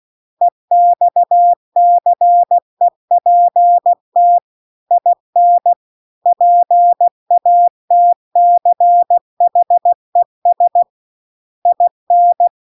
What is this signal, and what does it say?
Morse code, heard as a keyed tone, about 16 wpm, 700 Hz. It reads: EXCEPT IN PATCHES IN